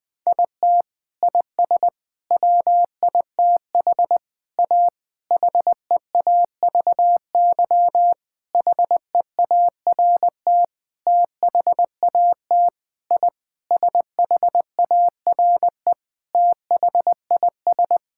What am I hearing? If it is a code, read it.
Morse code, heard as a keyed tone, about 20 wpm, 700 Hz. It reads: IT IS WITH A HEAVY HEART THAT I SHARE THIS
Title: Sentences with 8-Word Phrases - 20wpm